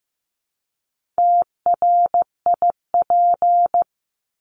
Morse code, heard as a keyed tone, about 15 wpm, 700 Hz. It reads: TRIP
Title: Top 1251-1500 Words 15wpm